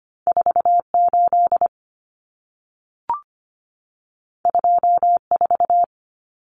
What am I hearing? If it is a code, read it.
Morse code, heard as a keyed tone, about 25 wpm, 700 Hz. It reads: 48 24
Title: Numbers - 2-Digit 25wpm